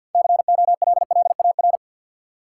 Morse code, heard as a keyed tone, about 48 wpm, 700 Hz. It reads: COLLAR
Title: Six Letter Words - 50wpm